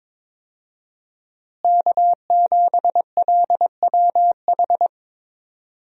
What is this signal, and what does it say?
Morse code, heard as a keyed tone, about 22 wpm, 700 Hz. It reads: K7LWH
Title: USA Call Signs - 5 Characters - Repeated 3x - 22wpm